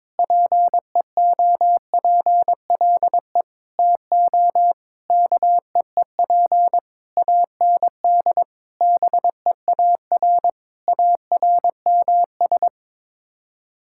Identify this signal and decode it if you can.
Morse code, heard as a keyed tone, about 22 wpm, 700 Hz. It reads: PEOPLE TO KEEP AND BEAR ARMS
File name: Sets of 9 Words - 22wpm